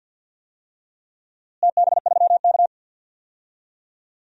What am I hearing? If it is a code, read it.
Morse code, heard as a keyed tone, about 49 wpm, 700 Hz. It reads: TB3K
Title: DXCC-100 Call Signs - 4 Characters - Repeated 3x - 50wpm